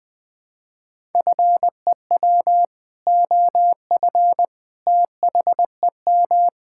Morse code, heard as a keyed tone, about 20 wpm, 700 Hz. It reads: FEW OF THEM
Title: Sentences with 2-Word Phrases - Spoken Beginning - 20wpm